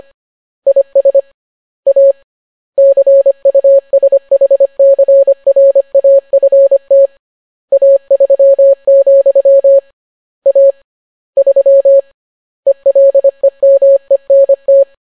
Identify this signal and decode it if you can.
Morse code, heard as a keyed tone, about 25 wpm, 550 Hz. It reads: IS A CUSHCRAFT A3, A 3 ELEMENT